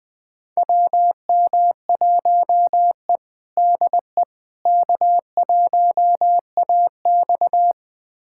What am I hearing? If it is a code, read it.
Morse code, heard as a keyed tone, about 20 wpm, 700 Hz. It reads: WM1E DE K1AX
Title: USA Callsign DE Callsign - 4 Characters - 20wpm